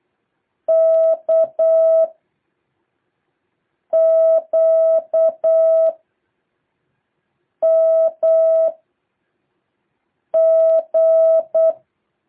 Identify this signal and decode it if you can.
Morse code, heard as a keyed tone, about 8 wpm, 650 Hz. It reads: K Q M G
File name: jh8cbh / 2023cw